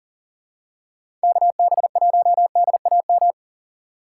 Morse code, heard as a keyed tone, about 40 wpm, 700 Hz. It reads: KB1DAM
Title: USA Call Signs - 6 Characters - 2x Character Spacing - 40wpm